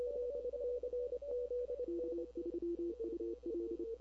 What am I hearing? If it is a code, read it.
Morse code, heard as a keyed tone, about 30 wpm, 340 Hz, with noise. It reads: K3UL
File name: CW / 2005-k5tr arrl cw dx -21:00